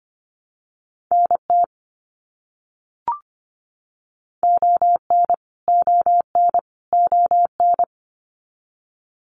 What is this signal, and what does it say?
Morse code, heard as a keyed tone, about 25 wpm, 700 Hz. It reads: NT ON ON ON
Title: Canadian Provinces and Territories - Repeated 3x - 25wpm